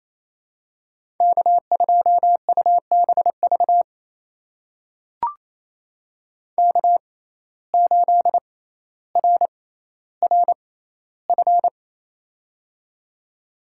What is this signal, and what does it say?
Morse code, heard as a keyed tone, about 28 wpm, spaced as for 14 wpm, 700 Hz. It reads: K2UBV K8RRF
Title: USA Call Signs - 5 Characters - 6x Character Spacing - 28wpm